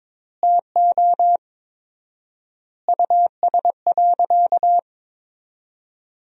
Morse code, heard as a keyed tone, about 23 wpm, 700 Hz. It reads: TO US.
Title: 4x Word-Spacing - Sentences from Top 200 Words - 22wpm